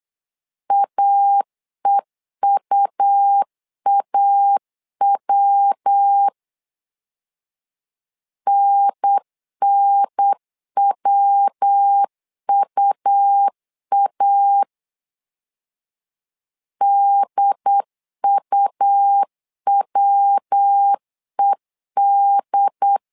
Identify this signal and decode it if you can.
Morse code, heard as a keyed tone, about 8 wpm, 800 Hz. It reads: AEUAW NNWUA DUWED
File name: LEZIONI / 3 MWRO / 3 MW 8wpm 2501